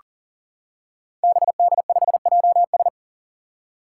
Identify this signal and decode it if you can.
Morse code, heard as a keyed tone, about 38 wpm, 700 Hz. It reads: BD5JS
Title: Five Challenging Letter-Number Combinations 40wpm